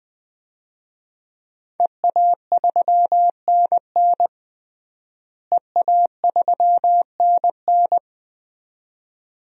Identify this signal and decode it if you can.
Morse code, heard as a keyed tone, about 20 wpm, 700 Hz. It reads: EA3NN EA3NN
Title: DXCC-100 Call Signs - 5 Characters - Repeated 2x - 20wpm